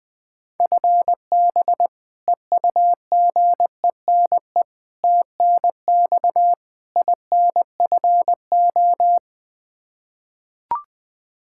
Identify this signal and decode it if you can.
Morse code, heard as a keyed tone, about 20 wpm, 700 Hz. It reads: FB EUGENE TNX INFO
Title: Practice QSO - Exchange 3 - More Info - 20wpm